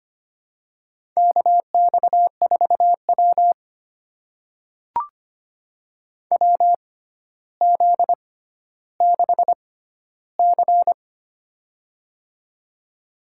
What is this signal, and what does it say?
Morse code, heard as a keyed tone, about 25 wpm, spaced as for 13 wpm, 700 Hz. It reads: KX4W WZ6C
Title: USA Call Signs - 4 Characters - 6x Character Spacing - 25wpm